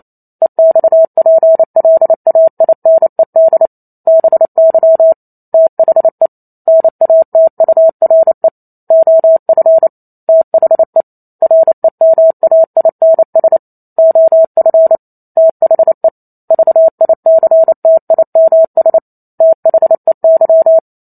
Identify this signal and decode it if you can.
Morse code, heard as a keyed tone, about 28 wpm, 650 Hz. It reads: EXPLAINED BY THE NATURE OF THE REMAINS OF THE VICTIMS THEY